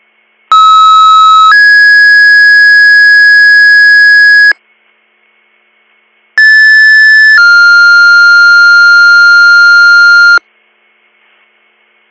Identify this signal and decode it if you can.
Two-tone page, heard as a one-second tone, then a three-second tone, about 1290 then 1650 Hz, about 1750 then 1400 Hz. It is heard from 0.5 to 10.4 s.